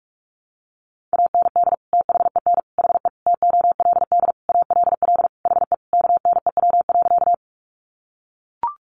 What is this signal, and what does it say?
Morse code, heard as a keyed tone, about 44 wpm, 700 Hz. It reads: AND THEN HE TOLD ALL HE KNEW.
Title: Sentences from Top 400 Words 45wpm